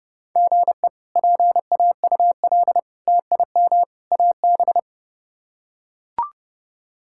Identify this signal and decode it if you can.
Morse code, heard as a keyed tone, about 30 wpm, 700 Hz. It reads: GE PAUL TIM AB
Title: K1USN SST QSO Practice - 30wpm